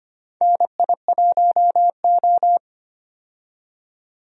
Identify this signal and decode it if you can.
Morse code, heard as a keyed tone, about 25 wpm, 700 Hz. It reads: NI1O
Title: USA Call Signs - 4 Characters - 25wpm